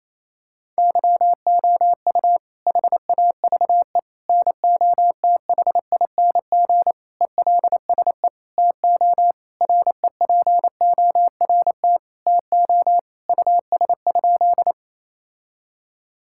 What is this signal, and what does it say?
Morse code, heard as a keyed tone, about 28 wpm, 700 Hz. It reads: YOU HAVE NOTHING ELSE TO REPORT TO US?
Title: Sentences from Top 1000 Words - Review Part 6 - 28wpm